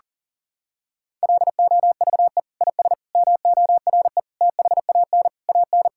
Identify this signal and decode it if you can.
Morse code, heard as a keyed tone, about 40 wpm, 700 Hz. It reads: LOVE IS MORE THAN AN
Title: Sentences from Top 1600 Words 40wpm